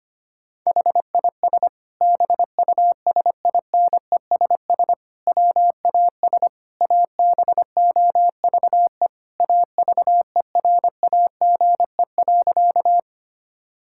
Morse code, heard as a keyed tone, about 25 wpm, 700 Hz. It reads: HIS BUSINESS WAS ABOVE AVERAGE.
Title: Sentences from Top 1000 Words - Review Part 6 - 25wpm